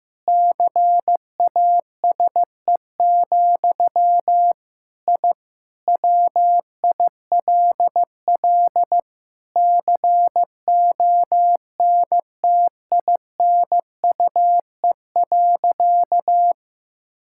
Morse code, heard as a keyed tone, about 15 wpm, 700 Hz. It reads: CASE, I WILL CONTINUE.
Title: Sentences from Top 1000 Words - Review Part 6 - 15wpm